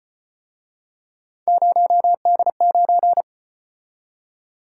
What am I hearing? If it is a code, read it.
Morse code, heard as a keyed tone, about 33 wpm, 700 Hz. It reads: 0D9